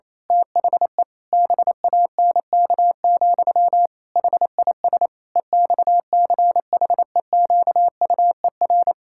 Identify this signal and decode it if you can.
Morse code, heard as a keyed tone, about 28 wpm, 700 Hz. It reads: THE BANK, HIS EXCHEQUER
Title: Pangram Challenge v2 - 28wpm